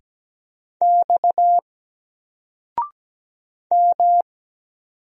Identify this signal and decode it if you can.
Morse code, heard as a keyed tone, about 17 wpm, 700 Hz. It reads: X M
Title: Single Letter-Number 17wpm